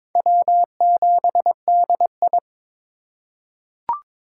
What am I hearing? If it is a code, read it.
Morse code, heard as a keyed tone, about 22 wpm, 700 Hz. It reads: W7DI
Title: USA Call Signs - 4 Characters - 3x Character Spacing - 22wpm